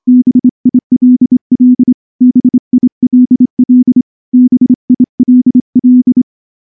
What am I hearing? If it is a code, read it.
Morse code, heard as a keyed tone, about 24 wpm, 250 Hz. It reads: BILL BILL BILL
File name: morse / morse-training-25WPM